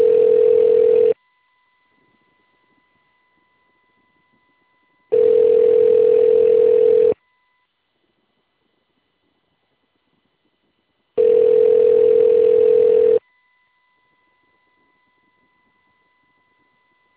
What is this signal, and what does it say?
Ringback tone (North America), 440 + 480 Hz.